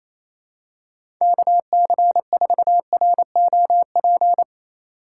Morse code, heard as a keyed tone, about 28 wpm, 700 Hz. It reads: KC4ROP
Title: USA Call Signs - 6 Characters - 2x Character Spacing - 28wpm